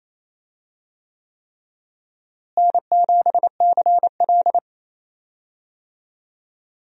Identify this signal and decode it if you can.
Morse code, heard as a keyed tone, about 28 wpm, 700 Hz. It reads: N7CL
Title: USA Call Signs - 4 Characters - Repeated 3x - 28wpm